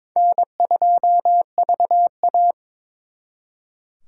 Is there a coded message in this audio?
Yes: Morse code from 0.2 to 2.5 s, heard as a keyed tone, about 28 wpm, 700 Hz. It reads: N2VA